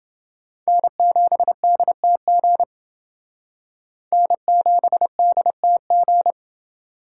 Morse code, heard as a keyed tone, about 29 wpm, 700 Hz. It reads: N7DTG N7DTG